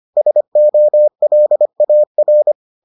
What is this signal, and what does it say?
Morse code, heard as a keyed tone, about 25 wpm, 600 Hz. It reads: SOLAR